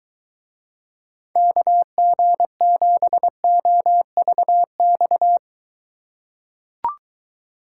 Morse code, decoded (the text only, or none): KG7OVX